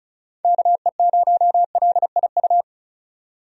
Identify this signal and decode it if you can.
Morse code, heard as a keyed tone, about 36 wpm, 700 Hz. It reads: KE0LIU
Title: USA Call Signs - 6 Characters - 35wpm